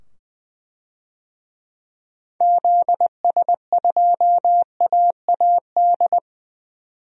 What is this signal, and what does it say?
Morse code, heard as a keyed tone, about 21 wpm, 700 Hz. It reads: ZS2AAD
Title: DXCC-100 Call Signs - 6 Characters - 20wpm